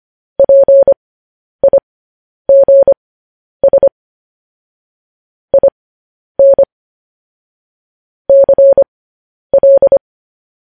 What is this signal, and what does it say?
Morse code, heard as a keyed tone, about 25 wpm, spaced as for 10 wpm, 550 Hz. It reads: PIGS IN CL
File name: podcast / News headlines-1x-25x10